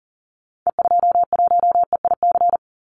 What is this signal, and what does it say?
Morse code, heard as a keyed tone, about 42 wpm, 700 Hz. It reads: E21EIC
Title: Call Signs - 40wpm